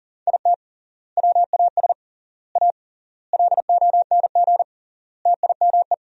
Morse code, heard as a keyed tone, about 40 wpm, 700 Hz. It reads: IT WAS A LONG TIME
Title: 3x Word-Spacing - Sentences from Top 500 Words - 40wpm